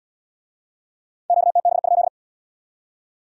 Morse code, heard as a keyed tone, about 68 wpm, 700 Hz. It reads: QTB?